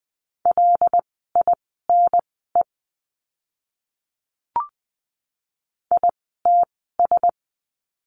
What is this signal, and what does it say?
Morse code, heard as a keyed tone, about 20 wpm, spaced as for 15 wpm, 700 Hz. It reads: LINE ITS
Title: 115 Lesson 19f - Letter w - Words 20wpm